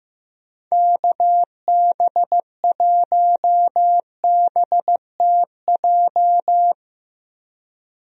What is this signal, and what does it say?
Morse code, heard as a keyed tone, about 15 wpm, 700 Hz. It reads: KB1BTJ
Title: USA Call Signs - 6 Characters - 15wpm